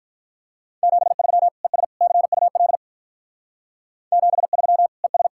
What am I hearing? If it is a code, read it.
Morse code, heard as a keyed tone, about 42 wpm, 700 Hz. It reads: 73 ES CUL 73 ES
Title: Speed-Racing Common QSO Phrases - 35wpm